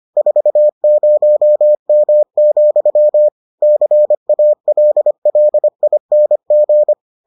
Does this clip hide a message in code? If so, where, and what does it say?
Yes: Morse code from 0.2 to 6.9 s, heard as a keyed tone, about 25 wpm, 600 Hz. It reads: 40M, CALLING